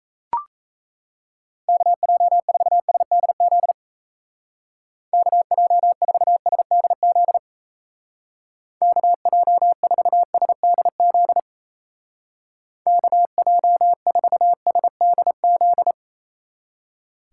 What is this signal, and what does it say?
Morse code, heard as a keyed tone, about 37 wpm, 700 Hz. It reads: KJ4SDZ KJ4SDZ KJ4SDZ KJ4SDZ